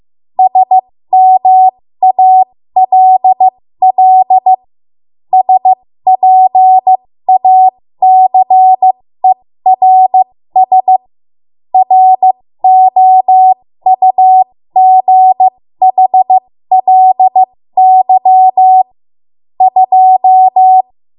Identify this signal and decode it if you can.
Morse code, heard as a keyed tone, about 15 wpm, spaced as for 13 wpm, 750 Hz. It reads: SMALL SPACERS ROUGHLY 2
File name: Morse / 13WPM